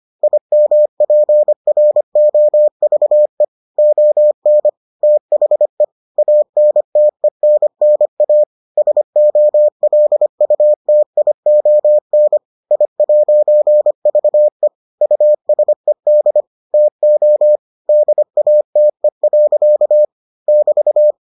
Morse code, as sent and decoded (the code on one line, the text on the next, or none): .. -- .--. .-. --- ...- . / --- -. / - .... / .- -. - . -. -. .- / ... --- .-.. ..- - .. --- -. / .. .----. ...- . / ..- ... -.. / - --- / -.. .- - . .-.-.- / -...-
IMPROVE ON THE ANTENNA SOLUTION I'VE USED TO DATE. <BT>